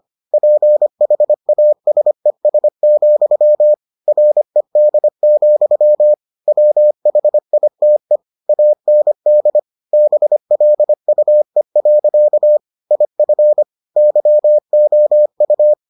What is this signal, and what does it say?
Morse code, heard as a keyed tone, about 25 wpm, 600 Hz. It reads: PHASES, RED, WHITE AND BLUE. IF YOU